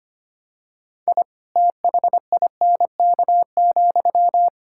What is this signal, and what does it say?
Morse code, heard as a keyed tone, about 25 wpm, 700 Hz. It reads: I THINK,
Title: Sentences from Top 1000 Words - Review Part 5 - 25wpm